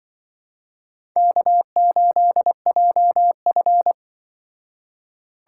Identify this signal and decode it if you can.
Morse code, heard as a keyed tone, about 25 wpm, 700 Hz. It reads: K8JF
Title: Speed Racing - USA 1x2 Call Signs 20wpm